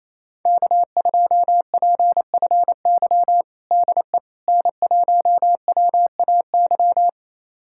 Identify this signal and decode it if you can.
Morse code, heard as a keyed tone, about 28 wpm, 700 Hz. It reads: K2PFY DE N1WAY
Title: USA Callsign DE Callsign 28wpm